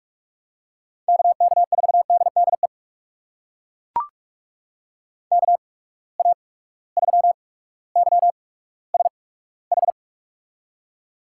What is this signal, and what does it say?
Morse code, heard as a keyed tone, about 41 wpm, spaced as for 19 wpm, 700 Hz. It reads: KK4DDE KA3YSH
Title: USA Call Signs - 6 Characters - 8x Character Spacing - 45wpm